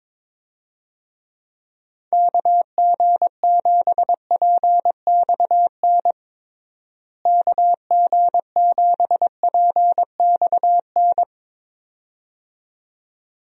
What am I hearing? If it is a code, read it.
Morse code, heard as a keyed tone, about 22 wpm, 700 Hz. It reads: KG7PXN KG7PXN
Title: USA Call Signs - 6 Characters - Repeated 2x - 22wpm